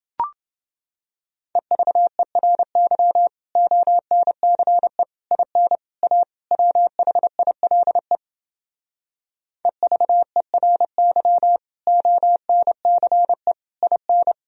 Morse code, heard as a keyed tone, about 29 wpm, 700 Hz. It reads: EVERY ONCE IN A WHILE EVERY ONCE IN